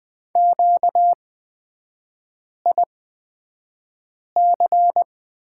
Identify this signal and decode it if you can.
Morse code, heard as a keyed tone, about 20 wpm, spaced as for 5 wpm, 700 Hz. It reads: QIC
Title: 236 Lesson 38e - Letter q - Three Characters 20wpm